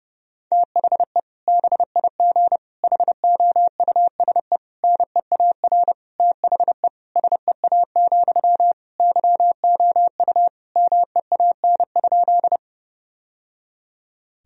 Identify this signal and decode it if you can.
Morse code, heard as a keyed tone, about 30 wpm, 700 Hz. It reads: THE BIG HOUSE NEAR THE SEA, YOU MEAN?
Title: Sentences from Top 1000 Words - Review Part 1 - 30wpm